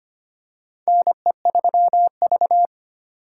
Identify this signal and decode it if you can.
Morse code, heard as a keyed tone, about 25 wpm, 700 Hz. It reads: NE3V
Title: USA Call Signs - 3x Character Spacing - 25wpm